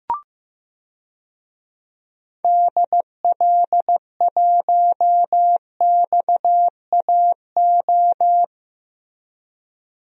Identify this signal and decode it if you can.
Morse code, heard as a keyed tone, about 15 wpm, 700 Hz. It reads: DL1XAO
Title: DXCC-100 Call Signs - 6 Characters - Repeated 2x - 15wpm